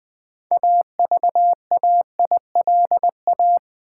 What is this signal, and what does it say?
Morse code, heard as a keyed tone, about 20 wpm, 700 Hz. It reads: AVAILA
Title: Build A Word - 6 Letters At A Time - 12-Letter Words - 20wpm